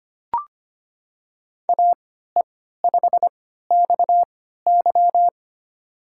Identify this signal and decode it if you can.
Morse code, heard as a keyed tone, about 25 wpm, spaced as for 14 wpm, 700 Hz. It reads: AE5XY
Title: USA Call Signs - 5 Characters - 3x Character Spacing - 25wpm